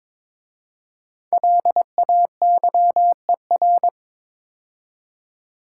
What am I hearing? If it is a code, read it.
Morse code, heard as a keyed tone, about 22 wpm, 700 Hz. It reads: LAYER
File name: 5 Letter Words - First 3 Letters Given - 22wpm